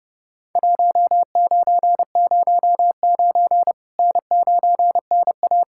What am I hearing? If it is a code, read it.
Morse code, heard as a keyed tone, about 32 wpm, 700 Hz. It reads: 1909 N9NA
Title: Worked All Europe DX Contest - QTC Exchange - 30wpm